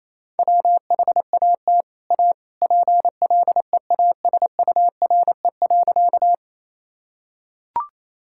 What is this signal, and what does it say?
Morse code, heard as a keyed tone, about 28 wpm, 700 Hz. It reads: WHAT A PLEASURE.
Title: Sentences from Top 1600 Words 28wpm